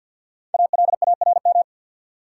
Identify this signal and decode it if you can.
Morse code, heard as a keyed tone, about 50 wpm, 700 Hz. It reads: ALARM
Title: Five Letter Words - 50wpm